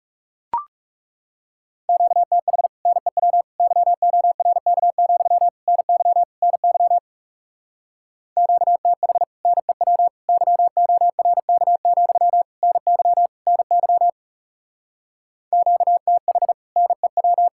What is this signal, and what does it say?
Morse code, heard as a keyed tone, about 36 wpm, 700 Hz. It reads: QTH NEW YORK, NY NY QTH NEW YORK, NY NY QTH NEW
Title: Speed-Racing Common QSO Phrases - 30wpm